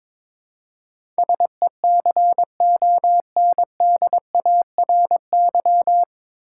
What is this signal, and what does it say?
Morse code, heard as a keyed tone, about 23 wpm, 700 Hz. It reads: SECONDARY
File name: 9 Letter Words - First 3 Letters Given - 22wpm